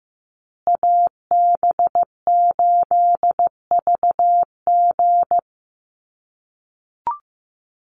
Morse code, heard as a keyed tone, about 15 wpm, 700 Hz. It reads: AB8VG
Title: USA Call Signs Repeated 3x 15wpm